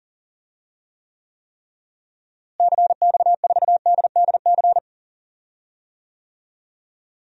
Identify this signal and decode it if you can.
Morse code, heard as a keyed tone, about 40 wpm, 700 Hz. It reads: CX4DDC
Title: DXCC-100 Call Signs - 6 Characters - Repeated 3x - 40wpm